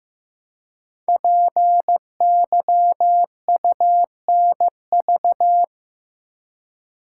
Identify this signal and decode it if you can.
Morse code, heard as a keyed tone, about 15 wpm, 700 Hz. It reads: PYUNV
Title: Code Groups - 5 Letters - 15wpm